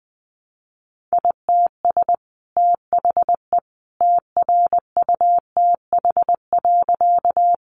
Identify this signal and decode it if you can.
Morse code, heard as a keyed tone, about 20 wpm, 700 Hz. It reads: ITS THE TRUTH.
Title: Sentences from Top 800 Words 20wpm